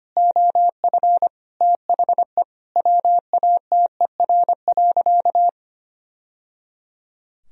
Morse code, heard as a keyed tone, about 25 wpm, 700 Hz. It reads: OF THE WATER.
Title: Sentences from Top 1000 Words - Review Part 3 - 25wpm